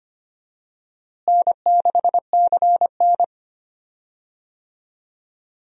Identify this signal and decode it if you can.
Morse code, heard as a keyed tone, about 25 wpm, 700 Hz. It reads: N6CN
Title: USA Call Signs - 4 Characters - Repeated 3x - 25wpm